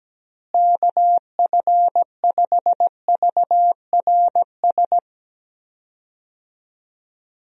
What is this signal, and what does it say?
Morse code, heard as a keyed tone, about 17 wpm, 700 Hz. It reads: KF5VRS
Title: USA Call Signs - 6 Characters - Repeated 2x - 17wpm